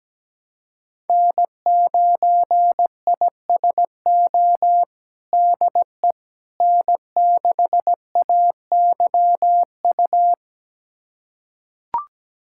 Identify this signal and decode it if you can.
Morse code, heard as a keyed tone, about 17 wpm, 700 Hz. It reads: N9ISO DE N6AYU
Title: USA Callsign DE Callsign - 5 Characters - 17wpm